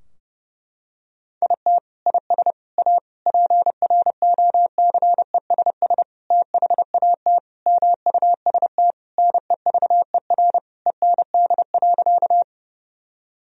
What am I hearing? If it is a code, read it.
Morse code, heard as a keyed tone, about 30 wpm, 700 Hz. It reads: IT IS A PROCESS THAT MUST NEVER END.
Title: Sentences from Top 1000 Words - Review Part 3 - 30wpm